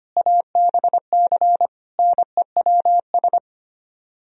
Morse code, heard as a keyed tone, about 32 wpm, 700 Hz. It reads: ABC NEWS